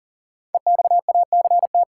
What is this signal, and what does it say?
Morse code, heard as a keyed tone, about 40 wpm, 700 Hz. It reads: EXACT
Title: Sentences from Top 1300 Words - New Word Drill Alt - 40wpm